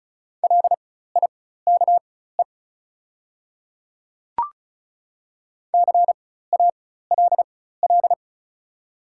Morse code, heard as a keyed tone, about 34 wpm, spaced as for 16 wpm, 700 Hz. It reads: LIKE CALL
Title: 4x Character Spacing - Top 200 Words - 35wpm